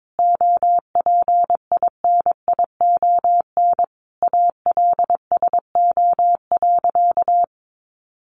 Morse code, heard as a keyed tone, about 22 wpm, 700 Hz. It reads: OPINION ALSO.
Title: Sentences from Top 1200 Words 22wpm